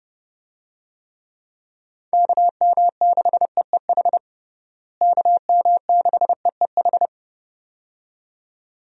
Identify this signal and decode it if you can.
Morse code, heard as a keyed tone, about 32 wpm, 700 Hz. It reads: KM6EEH KM6EEH
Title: USA Call Signs - 6 Characters - Repeated 2x - 30wpm